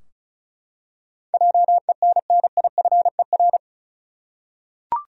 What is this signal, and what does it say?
Morse code, heard as a keyed tone, about 36 wpm, 700 Hz. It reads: JENNIFER
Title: USA Common Names - 35wpm